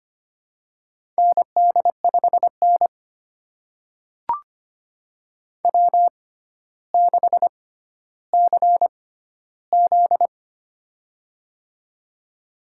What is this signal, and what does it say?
Morse code, heard as a keyed tone, about 25 wpm, spaced as for 13 wpm, 700 Hz. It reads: ND5N W6CZ
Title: USA Call Signs - 4 Characters - 6x Character Spacing - 25wpm